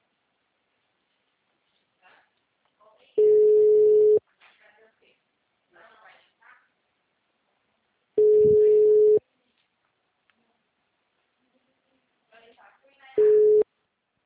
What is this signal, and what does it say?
Ringback tone (Europe, 425 Hz), with speech.
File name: out- -2014- - - .46